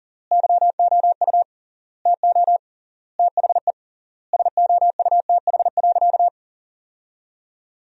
Morse code, heard as a keyed tone, about 37 wpm, 700 Hz. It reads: YOU TO THE SOUTH.
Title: 3x Word-Spacing - Sentences from Top 400 Words - 40wpm